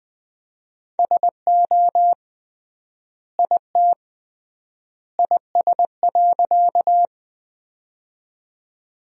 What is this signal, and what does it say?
Morse code, heard as a keyed tone, about 21 wpm, 700 Hz. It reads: SO IT IS.
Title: 3x Word-Spacing - Sentences from Top 100 Words - 20wpm